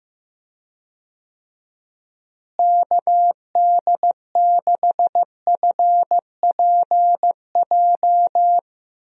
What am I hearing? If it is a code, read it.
Morse code, heard as a keyed tone, about 15 wpm, 700 Hz. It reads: KD6FPJ
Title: USA Call Signs - 6 Characters - Repeated 3x - 15wpm